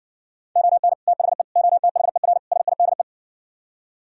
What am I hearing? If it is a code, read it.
Morse code, heard as a keyed tone, about 69 wpm, 700 Hz. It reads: ON THE OTHER SIDE